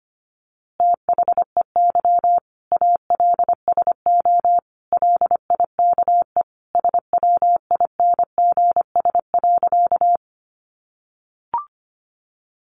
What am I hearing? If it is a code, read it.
Morse code, heard as a keyed tone, about 25 wpm, 700 Hz. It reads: THEY ALSO LIKE SWINGS.
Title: Easy Phrases 25wpm